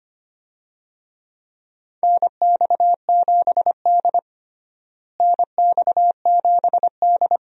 Morse code, heard as a keyed tone, about 25 wpm, 700 Hz. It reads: NX7D NX7D
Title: USA Call Signs - 4 Characters - Repeated 2x - 25wpm